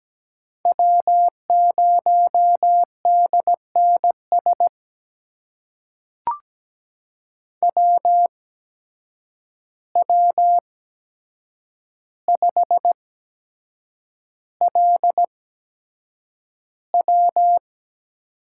Morse code, heard as a keyed tone, about 17 wpm, spaced as for 7 wpm, 700 Hz. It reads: W0DNS WW5LW
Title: USA Call Signs - 5 Characters - 8x Character Spacing - 17wpm